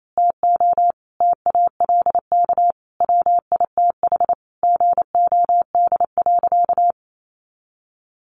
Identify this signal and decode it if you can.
Morse code, heard as a keyed tone, about 28 wpm, 700 Hz. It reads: TO TALK WITH GOD.
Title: A Sound Of Thunder 28wpm